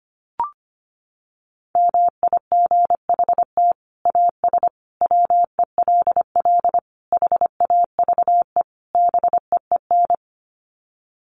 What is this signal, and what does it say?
Morse code, heard as a keyed tone, about 25 wpm, 700 Hz. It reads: MIGHT AS WELL HAVE BEEN
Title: Sets of 5 Words Encore2 25wpm